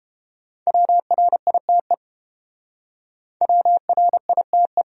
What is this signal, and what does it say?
Morse code, heard as a keyed tone, about 31 wpm, 700 Hz. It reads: WRITE WRITE